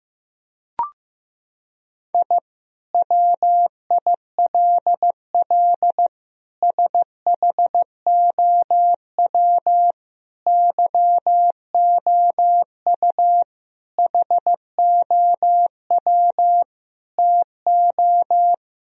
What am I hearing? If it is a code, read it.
Morse code, heard as a keyed tone, about 15 wpm, 700 Hz. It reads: I WILL SHOW YOU HOW TO
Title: Sentences from Top 1000 Words - Review Part 1 - 15wpm